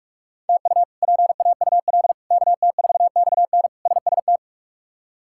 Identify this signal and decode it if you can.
Morse code, heard as a keyed tone, about 46 wpm, 700 Hz. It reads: TU PAUL KT4XN SST
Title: K1USN SST QSO Practice - 45wpm